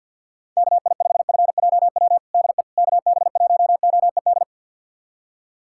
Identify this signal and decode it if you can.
Morse code, heard as a keyed tone, about 47 wpm, 700 Hz. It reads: KI5FJW DE KB1OED
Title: USA Callsign DE Callsign - 6 Characters - 50wpm